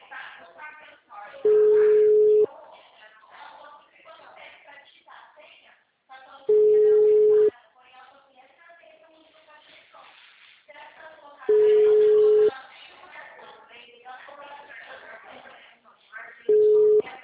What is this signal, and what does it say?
Ringback tone (Europe, 425 Hz), with speech.